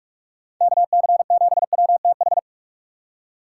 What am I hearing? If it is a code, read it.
Morse code, heard as a keyed tone, about 46 wpm, 700 Hz. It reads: KC7WTH
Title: USA Call Signs - 6 Characters - 3x Character Spacing - 45wpm